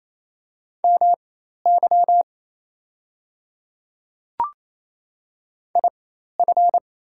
Morse code, heard as a keyed tone, about 28 wpm, spaced as for 13 wpm, 700 Hz. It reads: MY IF
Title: 4x Character Spacing - Top 100 Words - 28wpm